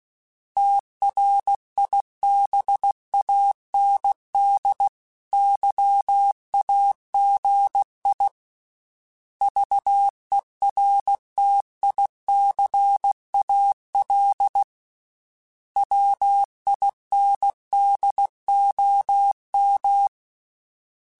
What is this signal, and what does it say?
Morse code, heard as a keyed tone, about 16 wpm, 800 Hz. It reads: TRIBAND YAGI VERTICAL WINDOM
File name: fromSMs / FromSMs mp3s / Antennas@15